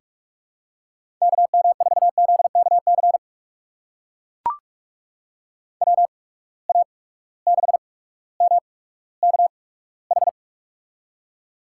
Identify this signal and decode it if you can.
Morse code, heard as a keyed tone, about 43 wpm, spaced as for 19 wpm, 700 Hz. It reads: KM4ZKC WA6MKH
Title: USA Call Signs - 6 Characters - 8x Character Spacing - 45wpm